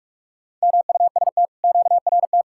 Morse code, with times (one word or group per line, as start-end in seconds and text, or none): 0.6-1.5 MUST
1.6-2.4 QRT